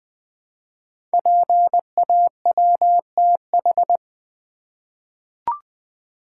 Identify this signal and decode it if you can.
Morse code, heard as a keyed tone, about 20 wpm, 700 Hz. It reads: PAWTH
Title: Code Groups - 5 Letters - 20wpm